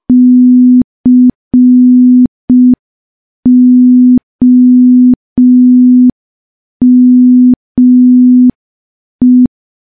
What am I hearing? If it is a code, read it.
Morse code, heard as a keyed tone, about 5 wpm, 250 Hz. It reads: COME
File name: morse / morse-training-05WPM